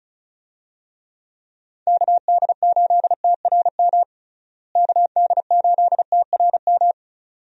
Morse code, heard as a keyed tone, about 34 wpm, 700 Hz. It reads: KD8TRM KD8TRM
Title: USA Call Signs - 6 Characters - Repeated 2x - 35wpm